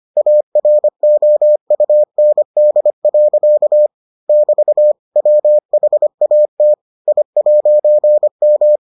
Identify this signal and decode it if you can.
Morse code, heard as a keyed tone, about 25 wpm, 600 Hz. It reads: AROUND. <BT> WHAT I'M